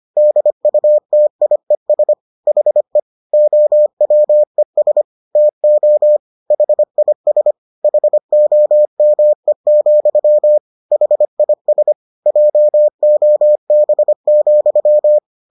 Morse code, heard as a keyed tone, about 25 wpm, 600 Hz. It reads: DUTIES HE OWES TO HIS HOME, HIS JOB,